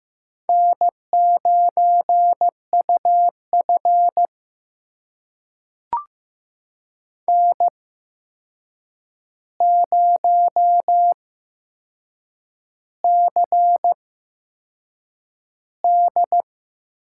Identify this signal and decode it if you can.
Morse code, heard as a keyed tone, about 15 wpm, spaced as for 6 wpm, 700 Hz. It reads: N9UF N0CD
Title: USA Call Signs - 4 Characters - 8x Character Spacing - 15wpm